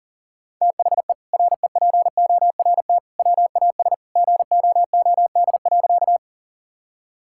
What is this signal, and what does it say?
Morse code, heard as a keyed tone, about 40 wpm, 700 Hz. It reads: THE REPORT WAS GOOD.
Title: Sentences from Top 1000 Words - Review Part 6 - 40wpm